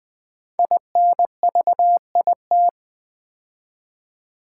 Morse code, heard as a keyed tone, about 20 wpm, 700 Hz. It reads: INVIT
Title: Build A Word - 5 Letters At A Time - 10-Letter Words - 20wpm